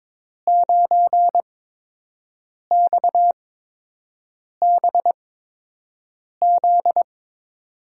Morse code, heard as a keyed tone, about 22 wpm, spaced as for 6 wpm, 700 Hz. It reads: 9XBZ
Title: USA Call Signs - 5 Characters - 8x Character Spacing - 22wpm